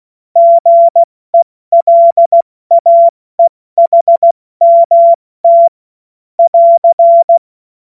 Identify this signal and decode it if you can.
Morse code, heard as a keyed tone, about 16 wpm, 680 Hz. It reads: GELAEHMT <AR>
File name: Morsekurs / Witze / Witz 07. Februar 2023 70 F80